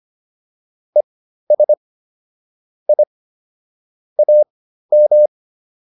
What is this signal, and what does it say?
Morse code, heard as a keyed tone, about 25 wpm, spaced as for 13 wpm, 600 Hz. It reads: ES I AM